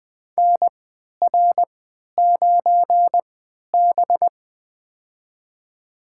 Morse code, heard as a keyed tone, about 20 wpm, spaced as for 11 wpm, 700 Hz. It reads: NR9B
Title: USA Call Signs - 3x Character Spacing - 20wpm